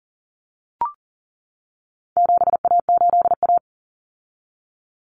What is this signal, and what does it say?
Morse code, heard as a keyed tone, about 40 wpm, 700 Hz. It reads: 7A8A
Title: Common Cut Numbers - 4-Digit 40wpm